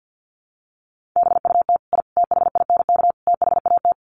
Morse code, heard as a keyed tone, about 50 wpm, 700 Hz. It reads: BUT I THINK THAT
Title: Sets of 4 Words 50wpm